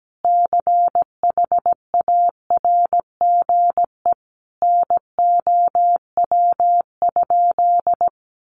Morse code, heard as a keyed tone, about 17 wpm, 700 Hz. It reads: CHARGE NOW?
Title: Sentences from Top 1200 Words 17wpm